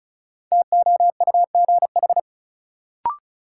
Morse code, heard as a keyed tone, about 36 wpm, 700 Hz. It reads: TOUGH
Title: Sentences from Top 1200 Words - New Word Drill Alt - 35wpm